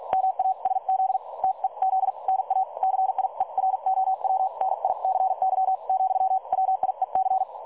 Morse code, heard as a keyed tone, about 48 wpm, spaced as for 40 wpm, 760 Hz. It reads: MASZ TEZ NA CIEROWNICY, WIEC
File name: dlazaawansowanych / nagraniazpasma / websdr recording 3565.0kHz